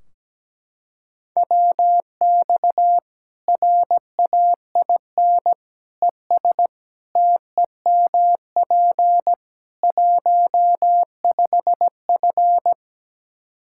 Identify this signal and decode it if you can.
Morse code, heard as a keyed tone, about 17 wpm, 700 Hz. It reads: WX RAIN ES TEMP 15F
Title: Practice QSO - Exchange 2 - Info - 17wpm